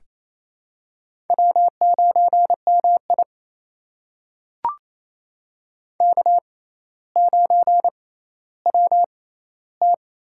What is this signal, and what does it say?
Morse code, heard as a keyed tone, about 28 wpm, spaced as for 14 wpm, 700 Hz. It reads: W9MI K9WT
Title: USA Call Signs - 4 Characters - 6x Character Spacing - 28wpm